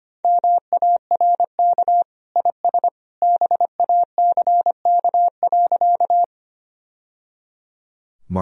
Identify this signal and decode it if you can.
Morse code, heard as a keyed tone, about 25 wpm, 700 Hz. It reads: MARK IS BACK.
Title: Sentences from Top 1000 Words - Review Part 1 - 25wpm